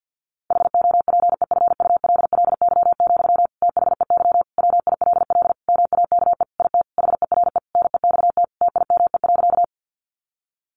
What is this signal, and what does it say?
Morse code, heard as a keyed tone, about 50 wpm, 700 Hz. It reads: HOPEFULLY, THEY WILL MAKE IT HERE NEXT TIME.
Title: Sentences from Top 1500 Words 50wpm